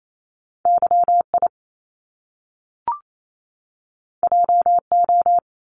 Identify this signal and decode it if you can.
Morse code, heard as a keyed tone, about 28 wpm, 700 Hz. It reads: YI JO
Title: Top Two Letter Combinations in Words 28wpm